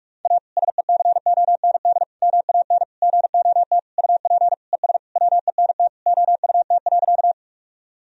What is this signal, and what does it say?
Morse code, heard as a keyed tone, about 43 wpm, 700 Hz. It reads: A SECOND MAN GOT UP ES WENT OUT.
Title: Sentences from Top 300 Words and Abbreviations 45wpm